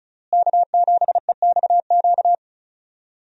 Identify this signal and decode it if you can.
Morse code, heard as a keyed tone, about 34 wpm, 700 Hz. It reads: K7EXQ
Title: USA Call Signs - 5 Characters - Repeated 3x - 35wpm